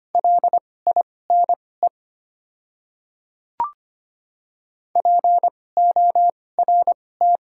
Morse code, heard as a keyed tone, about 25 wpm, spaced as for 18 wpm, 690 Hz. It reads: LINE PORT
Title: 2x Character Spacing - Top 200 Words - 25wpm